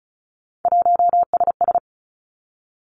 Morse code, heard as a keyed tone, about 36 wpm, 700 Hz. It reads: 1SS
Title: Three Challenging Letter-Number Combinations 35wpm